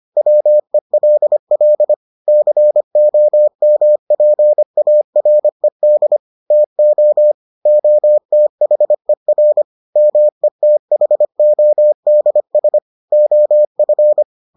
Morse code, heard as a keyed tone, about 25 wpm, 600 Hz. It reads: WELL COMPARED TO OTHER METHODS OF